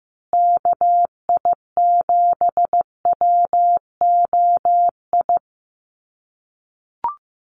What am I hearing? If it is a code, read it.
Morse code, heard as a keyed tone, about 15 wpm, 700 Hz. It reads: KI7WOI
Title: USA Call Signs Repeated 2x 15wpm